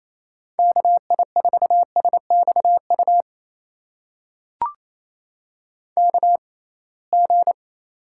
Morse code, heard as a keyed tone, about 28 wpm, spaced as for 21 wpm, 700 Hz. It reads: KI4SXU KG
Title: USA Call Signs - 6x Character Spacing - 28wpm